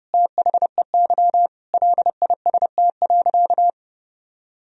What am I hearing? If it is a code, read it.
Morse code, heard as a keyed tone, about 32 wpm, 700 Hz. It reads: THEY LIST.
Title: Sentences from Top 1000 Words - Review Part 2 - 30wpm